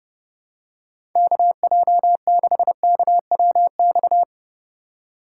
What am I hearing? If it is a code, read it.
Morse code, heard as a keyed tone, about 30 wpm, 700 Hz. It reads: KJ6KWX
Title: USA Call Signs - 6 Characters - 2x Character Spacing - 30wpm